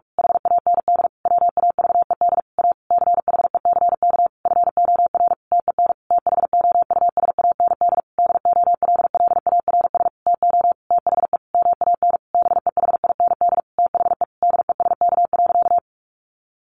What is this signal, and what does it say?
Morse code, heard as a keyed tone, about 44 wpm, 700 Hz. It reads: HAND WAVED A CHECK FOR TEN THOUSAND DOLLARS TO THE MAN BEHIND THE DESK.
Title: A Sound Of Thunder 45wpm